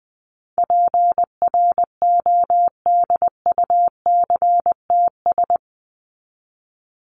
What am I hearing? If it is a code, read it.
Morse code, heard as a keyed tone, about 20 wpm, 700 Hz. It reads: PRODUCTS
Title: ES:S Suffix - Top 100 Words with Base Words 20wpm